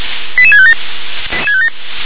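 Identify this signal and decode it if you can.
Five-tone selective call (ZVEI), heard as quick runs of five tones, tones of about 70 ms. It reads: 88657
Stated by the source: Five-tone ZVEI selective call 88655, received on an SDR.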